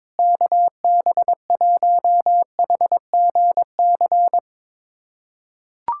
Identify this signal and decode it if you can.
Morse code, heard as a keyed tone, about 22 wpm, 700 Hz. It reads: KB1HGC
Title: USA Callsign DE Callsign 22wpm